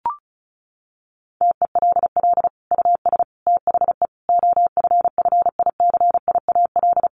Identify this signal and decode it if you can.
Morse code, heard as a keyed tone, about 34 wpm, 700 Hz. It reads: TELL US THE OFFICIAL
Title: Sentences from Top 1400 Words 35wpm